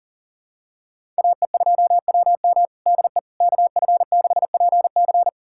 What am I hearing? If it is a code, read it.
Morse code, heard as a keyed tone, about 37 wpm, 700 Hz. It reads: AE2WM DE KF6PC
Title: USA Callsign DE Callsign - 5 Characters - 40wpm